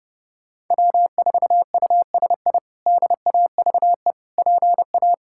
Morse code, heard as a keyed tone, about 30 wpm, 700 Hz. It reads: W4USI DAVE PA
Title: K1USN SST QSO Practice - 30wpm